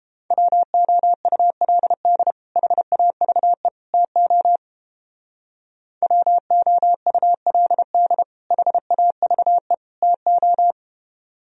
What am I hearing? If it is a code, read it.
Morse code, heard as a keyed tone, about 31 wpm, 700 Hz. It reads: WOULD HAVE TO WOULD HAVE TO